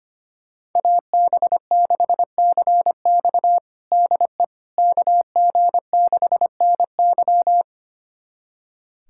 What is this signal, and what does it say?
Morse code, heard as a keyed tone, about 25 wpm, 700 Hz. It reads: AB6CX DE KG6NY
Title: USA Callsign DE Callsign - 5 Characters - 25wpm